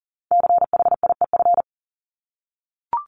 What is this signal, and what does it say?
Morse code, heard as a keyed tone, about 40 wpm, 700 Hz. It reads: CHIEF
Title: LY Suffix - Top 100 Words with Base Words 40wpm